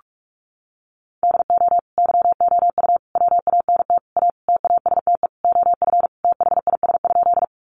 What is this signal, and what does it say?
Morse code, heard as a keyed tone, about 44 wpm, 700 Hz. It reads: DO YOU WANT A TASTE OF THIS?
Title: Sentences from Top 1100 Words 45wpm